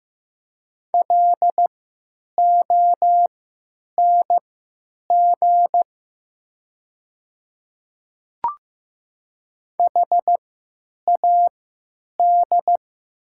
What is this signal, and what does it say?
Morse code, heard as a keyed tone, about 15 wpm, spaced as for 9 wpm, 690 Hz. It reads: LONG HAD
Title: 3x Character Spacing - Top 100 Words - 15wpm